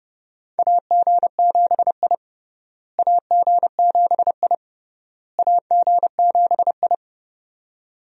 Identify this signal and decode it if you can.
Morse code, heard as a keyed tone, about 30 wpm, 700 Hz. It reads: AG7I AG7I AG7I
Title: USA Call Signs - 4 Characters - Repeated 3x - 30wpm